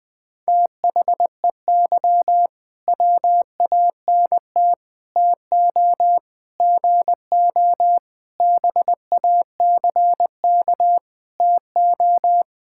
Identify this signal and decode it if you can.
Morse code, heard as a keyed tone, about 21 wpm, 700 Hz. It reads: THEY WANT TO GO BACK TO
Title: Sentences with 5-Word Phrases - Spoken Beginning - 20wpm